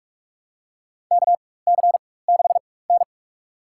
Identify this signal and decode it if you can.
Morse code, heard as a keyed tone, about 46 wpm, spaced as for 21 wpm, 700 Hz. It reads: KC6N
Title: USA Call Signs - 4 Characters - 4x Character Spacing - 45wpm